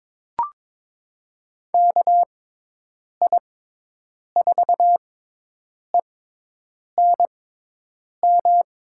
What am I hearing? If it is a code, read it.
Morse code, heard as a keyed tone, about 22 wpm, spaced as for 8 wpm, 700 Hz. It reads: KI4ENM
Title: USA Call Signs - 6 Characters - 6x Character Spacing - 22wpm